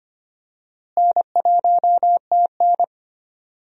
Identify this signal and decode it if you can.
Morse code, heard as a keyed tone, about 25 wpm, 700 Hz. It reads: N1TN
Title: USA Call Signs - 4 Characters - Repeated 2x - 25wpm